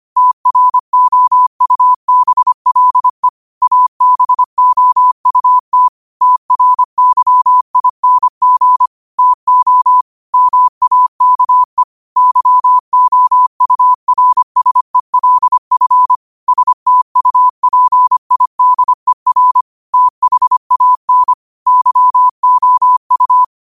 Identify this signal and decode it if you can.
Morse code, heard as a keyed tone, about 25 wpm, 1000 Hz. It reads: TROUBLE ABOUT TRYING TO MAKE YOURSELF STUPIDER THAN YOU
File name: qotdfiles / 25wpm